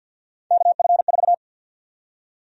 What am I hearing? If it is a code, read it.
Morse code, heard as a keyed tone, about 52 wpm, 700 Hz. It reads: KF4
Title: USA Call Sign Prefixes - 50wpm